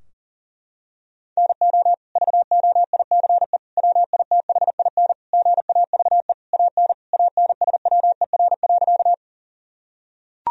Morse code, heard as a keyed tone, about 40 wpm, 700 Hz. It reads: NO VOICE WITHIN GAVE AN ANSWER.
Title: Sentences from Top 1000 Words - Review Part 4 - 40wpm